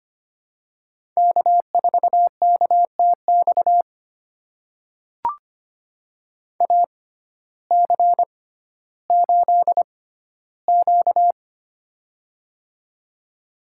Morse code, heard as a keyed tone, about 25 wpm, spaced as for 14 wpm, 700 Hz. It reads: K4KTX AC8Q
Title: USA Call Signs - 6x Character Spacing - 25wpm